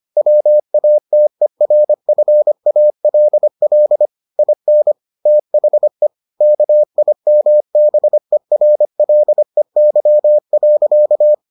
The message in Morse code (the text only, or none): WATERFALL IN THE KIMBERLEY.